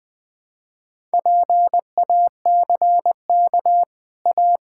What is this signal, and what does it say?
Morse code, heard as a keyed tone, about 20 wpm, 700 Hz. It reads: PACK A
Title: Easy Sentences with Hints - 20wpm